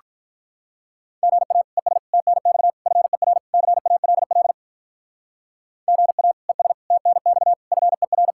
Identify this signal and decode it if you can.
Morse code, heard as a keyed tone, about 44 wpm, 700 Hz. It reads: GA ES TNX FER CALL GA ES TNX FER